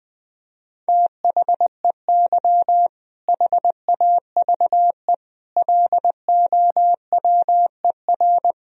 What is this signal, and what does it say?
Morse code, heard as a keyed tone, about 19 wpm, 700 Hz. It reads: THEY HAVE LOWER
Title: Sentences from Top 1000 Words - Review Part 4 - 20wpm